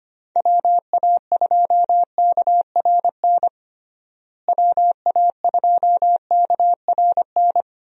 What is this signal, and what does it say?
Morse code, heard as a keyed tone, about 25 wpm, 700 Hz. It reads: WA2KRN WA2KRN
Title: USA Call Signs - 6 Characters - Repeated 2x - 25wpm